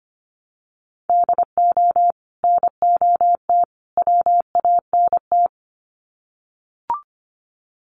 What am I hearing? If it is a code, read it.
Morse code, heard as a keyed tone, about 25 wpm, 700 Hz. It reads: DO NOT WANT
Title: Sets of 3 Words 25wpm